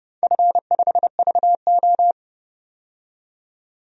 Morse code, heard as a keyed tone, about 32 wpm, 700 Hz. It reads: F5VO
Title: DXCC-100 Call Signs - 4 Characters - 30wpm